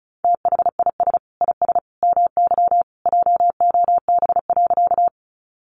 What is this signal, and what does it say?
Morse code, heard as a keyed tone, about 34 wpm, 700 Hz. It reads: THIS IS MY JOB.
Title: Sentences from Top 600 Words 35wpm